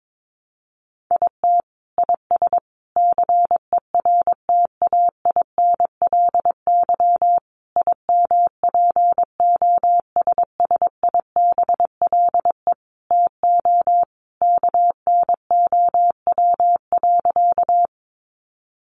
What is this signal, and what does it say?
Morse code, heard as a keyed tone, about 22 wpm, 700 Hz. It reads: IT IS CERTAINLY IMPOSSIBLE TO KNOW.
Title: Sentences from Top 1100 Words 22wpm